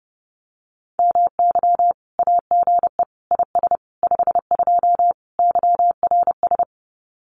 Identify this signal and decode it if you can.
Morse code, heard as a keyed tone, about 30 wpm, 700 Hz. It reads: MY AGE IS 52 YRS
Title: Common QSO Phrases Encore 30wpm